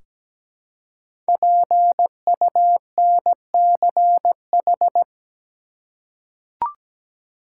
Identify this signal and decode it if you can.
Morse code, heard as a keyed tone, about 17 wpm, 700 Hz. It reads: PUNCH
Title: Five Letter Words - 17wpm